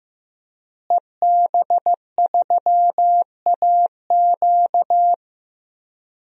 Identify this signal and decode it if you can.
Morse code, heard as a keyed tone, about 16 wpm, 700 Hz. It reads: EB3AQ
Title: DXCC-100 Call Signs - 5 Characters - Repeated 2x - 15wpm